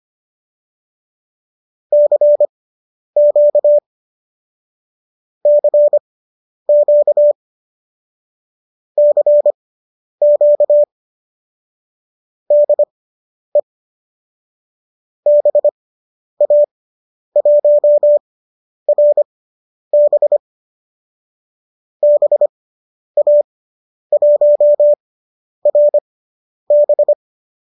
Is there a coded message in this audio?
Yes: Morse code from 1.9 to 27.1 s, heard as a keyed tone, about 25 wpm, spaced as for 10 wpm, 590 Hz. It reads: CQ CQ CQ DE BA1RB BA1RB